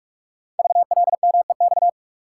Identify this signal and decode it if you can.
Morse code, heard as a keyed tone, about 44 wpm, 690 Hz. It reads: VLMEX
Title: Code Groups - 5 Letters - 45wpm